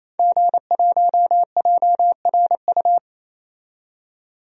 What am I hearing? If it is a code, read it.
Morse code, heard as a keyed tone, about 28 wpm, 700 Hz. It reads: G1JRU
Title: DXCC-100 Call Signs - 5 Characters - 28wpm